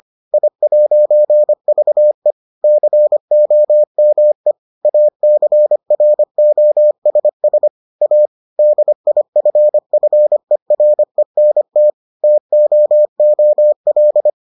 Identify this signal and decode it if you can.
Morse code, heard as a keyed tone, about 25 wpm, 600 Hz. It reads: I'VE COME ACROSS A DIFFERENT TOOL